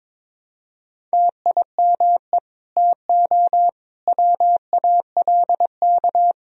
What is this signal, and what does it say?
Morse code, heard as a keyed tone, about 22 wpm, 700 Hz. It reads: TIME TO WALK
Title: Sentences from Top 1000 Words - Review Part 1 - 22wpm